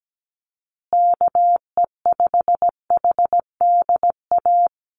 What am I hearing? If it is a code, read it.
Morse code, heard as a keyed tone, about 17 wpm, 700 Hz. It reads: KE5HDA
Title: USA Call Signs Repeated 2x 17wpm